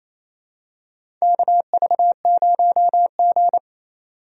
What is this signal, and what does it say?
Morse code, heard as a keyed tone, about 28 wpm, 700 Hz. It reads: KV0G
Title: USA Call Signs - 4 Characters - 28wpm